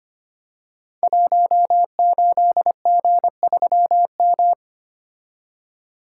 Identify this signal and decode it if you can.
Morse code, heard as a keyed tone, about 25 wpm, 700 Hz. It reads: 18G3M
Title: Code Groups - 5 Letter:Numbers - 25wpm